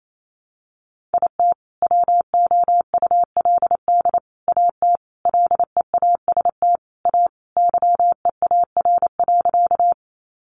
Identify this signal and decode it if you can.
Morse code, heard as a keyed tone, about 28 wpm, 700 Hz. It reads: IT WOULD AT LEAST A YEAR.
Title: Sentences from Top 700 Words 28wpm